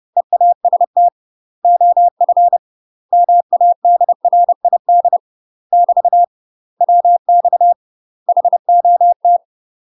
Morse code, heard as a keyed tone, about 30 wpm, 700 Hz. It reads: EAST OF MADRID <BT> WX HOT
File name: UK MASTER QSOs TRAINING / 30WPM / UK MASTER QSOs 0003